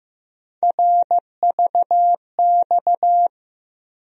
Morse code, heard as a keyed tone, about 15 wpm, 700 Hz. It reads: RVX
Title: USA Call Sign Postfixes - 3 Letters 15wpm